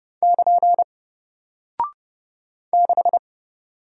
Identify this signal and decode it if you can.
Morse code, heard as a keyed tone, about 30 wpm, 700 Hz. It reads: <KN> 6
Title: Single Character (Letter:Number:Punctuation:Prosign) Alt 30wpm